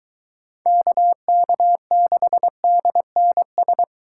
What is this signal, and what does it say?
Morse code, heard as a keyed tone, about 23 wpm, 700 Hz. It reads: KK6DNS